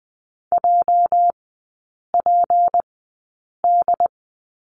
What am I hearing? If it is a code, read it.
Morse code, heard as a keyed tone, about 20 wpm, 700 Hz. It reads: J P D
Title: 205 Lesson 33e - Letter j - Three Characters 20wpm